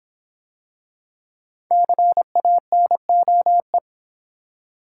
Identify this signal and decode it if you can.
Morse code, heard as a keyed tone, about 26 wpm, 700 Hz. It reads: CANOE